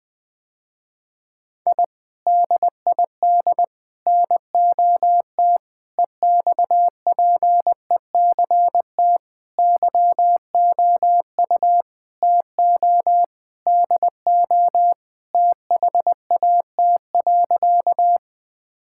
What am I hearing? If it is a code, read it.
Morse code, heard as a keyed tone, about 20 wpm, 700 Hz. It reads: I DID NOT EXPECT YOU TO DO THAT.
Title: Sentences with 3-Word Phrases - Spoken Beginning - 20wpm